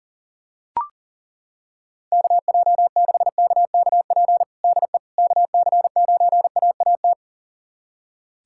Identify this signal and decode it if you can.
Morse code, heard as a keyed tone, about 40 wpm, 700 Hz. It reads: KJ6KKP DE KC9AAT
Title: USA Callsign DE Callsign - 6 Characters - 40wpm